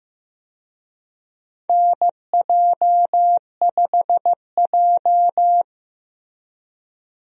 Morse code, heard as a keyed tone, about 15 wpm, 700 Hz. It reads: NJ5J
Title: USA Call Signs - 4 Characters - Repeated 3x - 15wpm